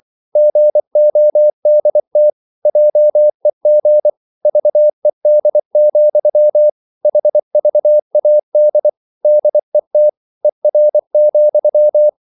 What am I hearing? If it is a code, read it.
Morse code, heard as a keyed tone, about 24 wpm, 600 Hz. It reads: GODT JEG VED, HVAD DET ER,
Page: Ferie 24 wpm